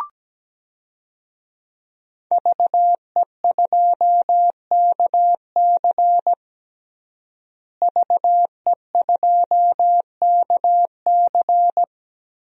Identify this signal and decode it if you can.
Morse code, heard as a keyed tone, about 17 wpm, 700 Hz. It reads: VE2KC VE2KC
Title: DXCC-100 Call Signs - 5 Characters - Repeated 3x - 17wpm